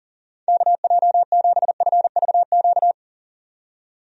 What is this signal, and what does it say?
Morse code, heard as a keyed tone, about 40 wpm, 690 Hz. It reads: KJ7FVQ
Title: USA Call Signs - 6 Characters - 40wpm